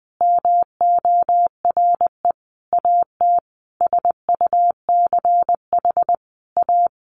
Morse code, heard as a keyed tone, about 20 wpm, 700 Hz. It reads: MORE AT SUCH A
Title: Sentences from Top 1100 Words 20wpm